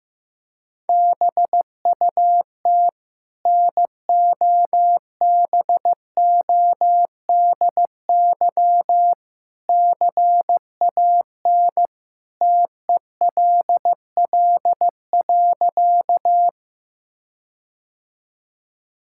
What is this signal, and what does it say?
Morse code, heard as a keyed tone, about 15 wpm, 700 Hz. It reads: BUT NOBODY CAN TELL.
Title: Sentences from Top 1600 Words 15wpm